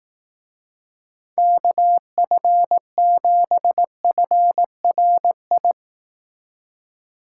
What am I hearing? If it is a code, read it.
Morse code, heard as a keyed tone, about 18 wpm, 700 Hz. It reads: KF7FRI
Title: Speed Racing - USA 2x3 Call Signs 15wpm